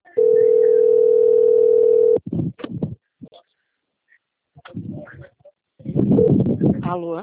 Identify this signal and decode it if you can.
Ringback tone (North America), 440 + 480 Hz, with speech.
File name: out- - - - - .22031